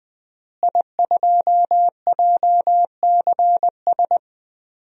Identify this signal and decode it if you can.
Morse code, heard as a keyed tone, about 21 wpm, 700 Hz. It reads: I2JCS
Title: DXCC-100 Call Signs - 5 Characters - Repeated 2x - 20wpm